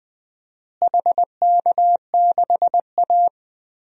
Morse code, heard as a keyed tone, about 21 wpm, 700 Hz. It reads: HK6A
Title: DXCC-100 Call Signs - 4 Characters - Repeated 2x - 20wpm